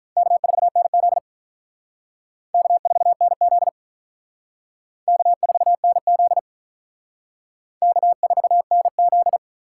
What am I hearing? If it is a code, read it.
Morse code, heard as a keyed tone, about 45 wpm, 700 Hz. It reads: K4NZ K4NZ K4NZ K4NZ